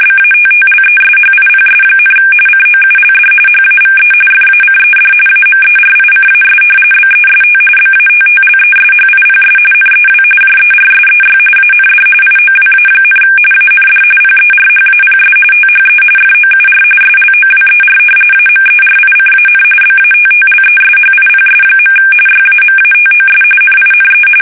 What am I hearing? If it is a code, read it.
RTTY, about 45.45 baud. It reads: BATTLESHIPS, COULD HAVE FACED AMERICAS BEST BATTLESHIPS NEW JERSEY AND IOWA, AND PERHAPS THE FOUR SLOWER AMERICAN BATTLESHIPS. YAMATOS NINE
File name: fsk850 / battleshipnj